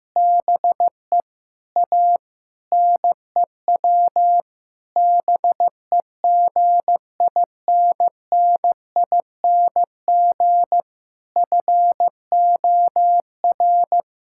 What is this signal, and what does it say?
Morse code, heard as a keyed tone, about 16 wpm, 700 Hz. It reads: BE A NEW BEGINNING FOR
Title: Sentences from Top 1000 Words - Review Part 5 - 15wpm